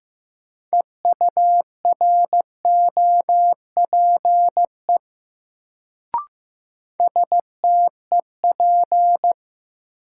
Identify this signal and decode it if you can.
Morse code, heard as a keyed tone, about 15 wpm, 700 Hz. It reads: EUROPE STEP
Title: Sets of 5 Words Encore - New Word Drill - 15wpm